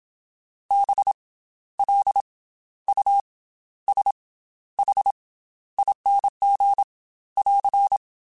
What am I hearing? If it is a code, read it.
Morse code, heard as a keyed tone, about 27 wpm, 800 Hz. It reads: B L U S H ING <AR>
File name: Special INGs / Special INGs-mp3s / Slow ing @25